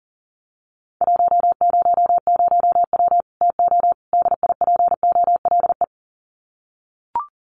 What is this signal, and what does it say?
Morse code, heard as a keyed tone, about 40 wpm, 700 Hz. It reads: 100W TO DIPOLE
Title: Common QSO Phrases 40wpm